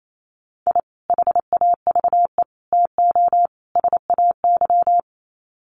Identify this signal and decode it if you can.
Morse code, heard as a keyed tone, about 28 wpm, 700 Hz. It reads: I HAVE TO SAY
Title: Sets of 4 Words 28wpm